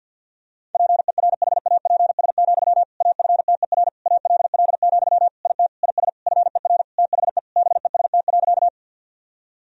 Morse code, heard as a keyed tone, about 52 wpm, 700 Hz. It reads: PERHAPS, AFTER ALL, IT IS FER THE BEST.
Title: Sentences from Top 500 Words and Abbreviations 50wpm